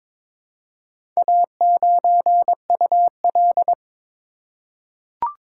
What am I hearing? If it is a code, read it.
Morse code, heard as a keyed tone, about 22 wpm, 690 Hz. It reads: A9UL
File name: practice / 22wpm / Four Letter-Number Combinations - 22wpm